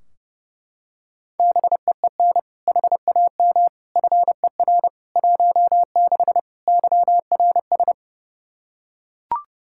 Morse code, heard as a keyed tone, about 29 wpm, 700 Hz. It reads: BEEN HAM FER 16 YRS
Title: Practice QSO - Exchange 3 - More Info - 30wpm